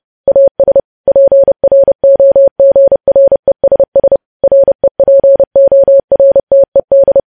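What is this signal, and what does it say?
Morse code, heard as a keyed tone, about 30 wpm, 550 Hz. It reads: AS PROGRESS REPORTED
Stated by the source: Morse news headlines sent at 30 wpm, opening with vvv.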